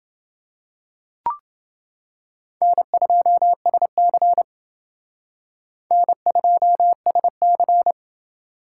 Morse code, heard as a keyed tone, about 28 wpm, 700 Hz. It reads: N2SC N2SC